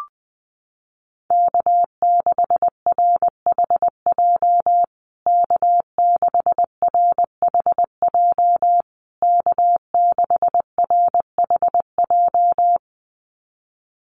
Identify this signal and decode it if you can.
Morse code, heard as a keyed tone, about 20 wpm, 700 Hz. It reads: K6RHJ K6RHJ K6RHJ
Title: USA Call Signs Repeated 3x 20wpm